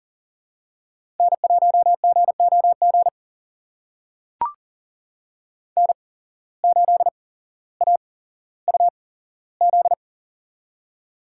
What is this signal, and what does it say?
Morse code, heard as a keyed tone, about 40 wpm, spaced as for 17 wpm, 700 Hz. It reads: N1GOG N8AUZ
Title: USA Call Signs - 5 Characters - 8x Character Spacing - 40wpm